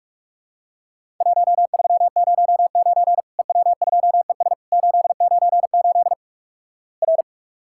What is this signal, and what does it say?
Morse code, heard as a keyed tone, about 46 wpm, 700 Hz. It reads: 1309 EW2ES 898 R
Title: Worked All Europe DX Contest - QTC Exchange - 45wpm